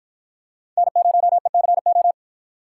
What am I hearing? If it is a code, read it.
Morse code, heard as a keyed tone, about 50 wpm, 700 Hz. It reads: N0ECO